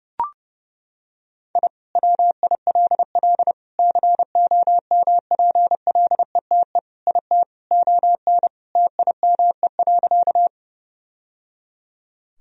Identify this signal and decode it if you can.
Morse code, heard as a keyed tone, about 30 wpm, 700 Hz. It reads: I WILL COMPLETE IT ON TIME.
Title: Sentences from Top 1000 Words - Review Part 2 - 30wpm